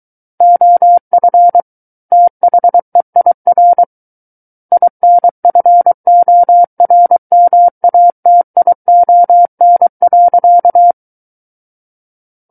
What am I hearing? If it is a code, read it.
Morse code, heard as a keyed tone, about 23 wpm, 700 Hz. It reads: OF THEIR INFORMATION.
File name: Book One / 14-Chapter 14 - 23 WPM